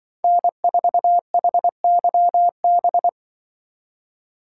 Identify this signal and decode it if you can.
Morse code, heard as a keyed tone, about 24 wpm, 700 Hz. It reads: N4HYB
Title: Speed Racing - USA 1x3 Call Signs 20wpm